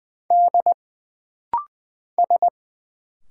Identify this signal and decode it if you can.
Morse code, heard as a keyed tone, about 20 wpm, 700 Hz. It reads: D S
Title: Single Letter-Number - Rapid-Fire 20wpm